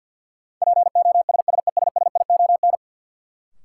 Morse code, heard as a keyed tone, about 53 wpm, 700 Hz. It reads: POSSESSION
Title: Sentences from Top 1400 Words - New Word Drill - 50wpm